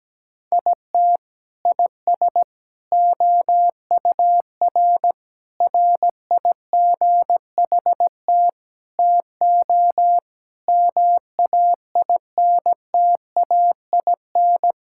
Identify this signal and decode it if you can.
Morse code, heard as a keyed tone, about 17 wpm, 700 Hz. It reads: IT IS OUR RIGHT TO MAINTAIN
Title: Sentences from Top 1000 Words - Review Part 5 - 17wpm